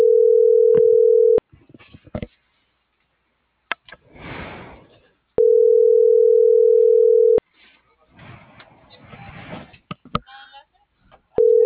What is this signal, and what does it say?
Ringback tone (North America), 440 + 480 Hz.